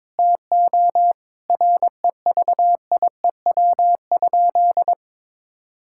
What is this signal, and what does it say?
Morse code, heard as a keyed tone, about 22 wpm, 700 Hz. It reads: TO REVIEW?
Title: Sentences from Top 1000 Words - Review Part 6 - 22wpm